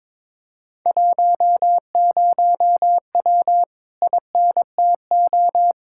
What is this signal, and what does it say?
Morse code, heard as a keyed tone, about 22 wpm, 700 Hz. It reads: 10W INTO
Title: Speed-Racing Common QSO Phrases - 22wpm